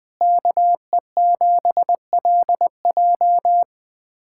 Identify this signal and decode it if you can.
Morse code, heard as a keyed tone, about 21 wpm, 700 Hz. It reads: KE7LJ
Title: USA Callsign DE Callsign - 5 Characters - 20wpm